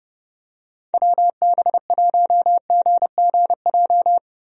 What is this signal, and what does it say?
Morse code, heard as a keyed tone, about 32 wpm, 700 Hz. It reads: WB1GGJ
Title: USA Call Signs - 6 Characters - Repeated 2x - 30wpm